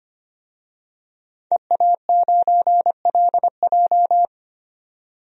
Morse code, heard as a keyed tone, about 25 wpm, 700 Hz. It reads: EA9LJ
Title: DXCC-100 Call Signs - 5 Characters - 25wpm